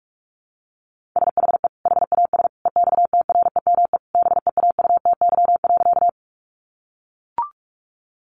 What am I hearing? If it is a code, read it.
Morse code, heard as a keyed tone, about 44 wpm, 700 Hz. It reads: SHE HAS EXTREME BEAUTY.
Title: Sentences from Top 1300 Words 45wpm